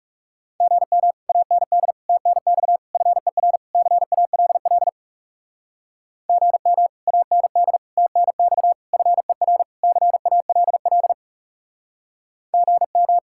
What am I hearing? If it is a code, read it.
Morse code, heard as a keyed tone, about 38 wpm, 700 Hz. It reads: GM AND TNX FER CALL GM AND TNX FER CALL GM